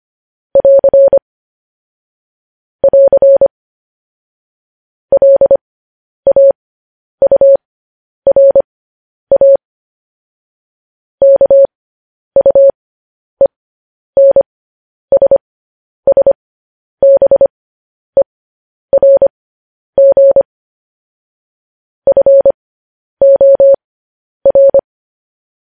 Morse code, heard as a keyed tone, about 25 wpm, spaced as for 10 wpm, 550 Hz. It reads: <AR> <AR> LAURA KUENSSBERG FOR
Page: news-headlines-in-morse-code-at-10-wpm